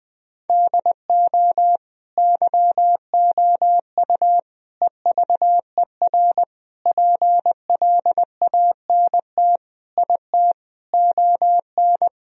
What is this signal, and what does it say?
Morse code, heard as a keyed tone, about 20 wpm, 700 Hz. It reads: DO YOU EVER PLANT IT ON
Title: Sentences from Top 1000 Words - Review Part 1 - 20wpm